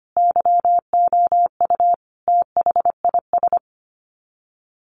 Morse code, heard as a keyed tone, about 25 wpm, 700 Hz. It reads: YOU THIS
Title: Sets of 5 Words 25wpm